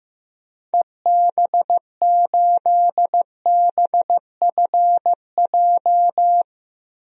Morse code, heard as a keyed tone, about 15 wpm, 700 Hz. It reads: EB8BFJ
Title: DXCC-100 Call Signs - 6 Characters - 15wpm